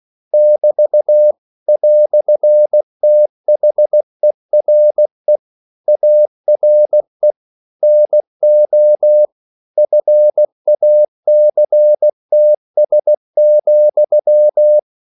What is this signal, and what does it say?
Morse code, heard as a keyed tone, about 16 wpm, 600 Hz. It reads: <BT> "THERE ARE NO FACTS,
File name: CW / Practice / EnglishQuotations0116